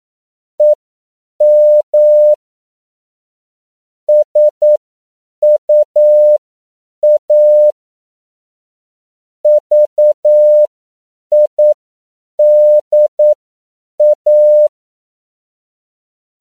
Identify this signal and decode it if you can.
Morse code, heard as a keyed tone, about 9 wpm, spaced as for 7 wpm, 600 Hz. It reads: EM SUA VIDA
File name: CW / 10PPM